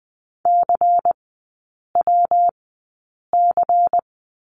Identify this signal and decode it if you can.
Morse code, heard as a keyed tone, about 20 wpm, spaced as for 8 wpm, 700 Hz. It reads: CWC
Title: 114 Lesson 19e - Letter w - Three Characters 20wpm